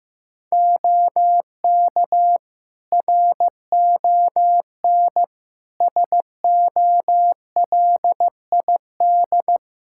Morse code, heard as a keyed tone, about 15 wpm, 700 Hz. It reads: OK RON SOLID
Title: Practice QSO - Exchange 3 - More Info - 15wpm